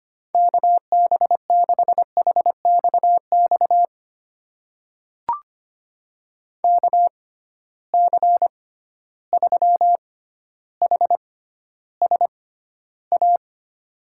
Morse code, heard as a keyed tone, about 25 wpm, spaced as for 13 wpm, 700 Hz. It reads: KB6HXX KC3HSA
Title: USA Call Signs - 6 Characters - 6x Character Spacing - 25wpm